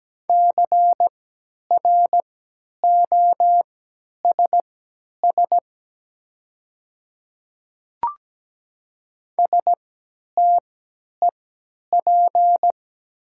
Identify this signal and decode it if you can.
Morse code, heard as a keyed tone, about 17 wpm, spaced as for 10 wpm, 700 Hz. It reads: CROSS STEP
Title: 3x Character Spacing - Top 500 Words - 17wpm